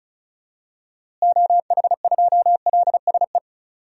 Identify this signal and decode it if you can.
Morse code, heard as a keyed tone, about 36 wpm, 700 Hz. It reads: OH2LSE
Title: DXCC-100 Call Signs - 6 Characters - Repeated 3x - 35wpm